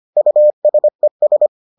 Morse code, heard as a keyed tone, about 25 wpm, 600 Hz. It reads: USES